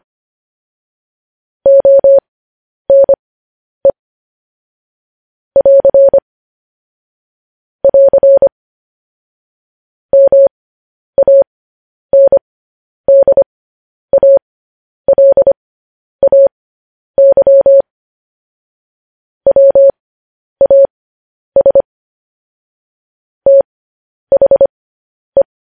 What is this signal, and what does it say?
Morse code, heard as a keyed tone, about 25 wpm, spaced as for 10 wpm, 550 Hz. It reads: ONE <AR> <AR> MANDALAY WAS THE
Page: News Headlines in Morse Code at 10 WPM